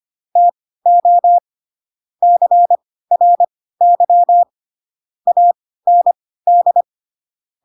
Morse code, heard as a keyed tone, about 25 wpm, spaced as for 16 wpm, 700 Hz. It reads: TO CRY AND